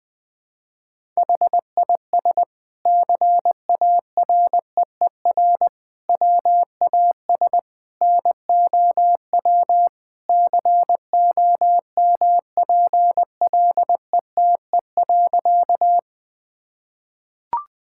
Morse code, heard as a keyed tone, about 20 wpm, 700 Hz. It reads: HIS CAREER WAS NOW COMPLETE.
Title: Sentences from Top 1000 Words - Review Part 6 - 20wpm